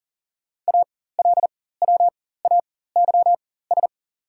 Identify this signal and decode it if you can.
Morse code, heard as a keyed tone, about 39 wpm, spaced as for 18 wpm, 700 Hz. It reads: ALWAYS
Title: 4x Character Spacing - Top 500 Words - 40wpm